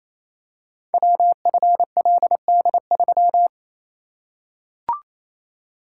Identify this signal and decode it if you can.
Morse code, heard as a keyed tone, about 27 wpm, 700 Hz. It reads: WFLD3
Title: Code Groups - 5 Letter:Numbers - 28wpm